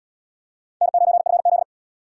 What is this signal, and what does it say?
Morse code, heard as a keyed tone, about 71 wpm, 700 Hz. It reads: N9FZ